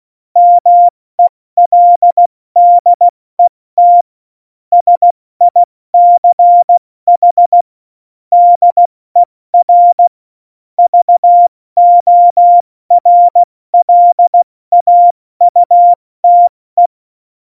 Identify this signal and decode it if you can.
Morse code, heard as a keyed tone, about 16 wpm, spaced as for 14 wpm, 700 Hz. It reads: MELDET SICH DER VORLAUTE
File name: Morsekurs / Witze / Witz 02. April 2019, 70 F80